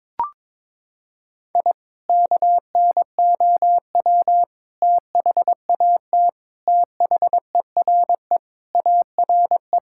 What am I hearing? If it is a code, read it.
Morse code, heard as a keyed tone, about 21 wpm, 700 Hz. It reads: I KNOW THAT THERE ARE
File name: practice / 22wpm / Sentences with 3-Word Phrases - 22wpm